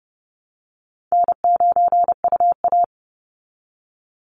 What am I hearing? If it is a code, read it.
Morse code, heard as a keyed tone, about 30 wpm, 700 Hz. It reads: N9UA
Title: Easy Call Signs Repeated 2x - 30wpm